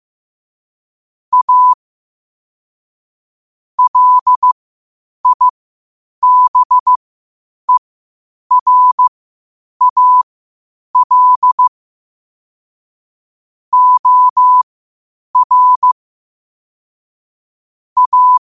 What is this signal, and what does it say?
Morse code, heard as a keyed tone, about 15 wpm, spaced as for 8 wpm, 1000 Hz. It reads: A LIBERAL OR A
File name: qotdfiles / 8wpm